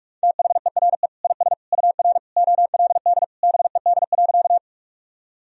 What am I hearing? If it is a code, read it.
Morse code, heard as a keyed tone, about 41 wpm, 700 Hz. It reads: THERE IS UR OLD BED.
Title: Sentences from Top 500 Words and Abbreviations 45wpm